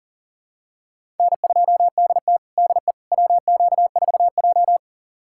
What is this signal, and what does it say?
Morse code, heard as a keyed tone, about 37 wpm, 700 Hz. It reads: N2DT DE WQ4J
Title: USA Callsign DE Callsign - 4 Characters - 40wpm